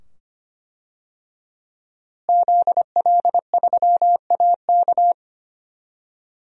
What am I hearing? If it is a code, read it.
Morse code, heard as a keyed tone, about 25 wpm, 700 Hz. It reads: ZL3AK
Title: DXCC-100 Call Signs - 5 Characters - Repeated 2x - 25wpm